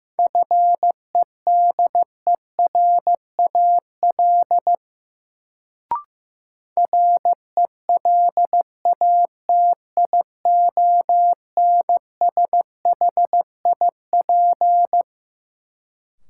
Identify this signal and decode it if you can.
Morse code, heard as a keyed tone, about 15 wpm, 700 Hz. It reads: FEDERAL RELATIONSHIP
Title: Sentences from Top 1100 Words - New Word Drill Alt - 15wpm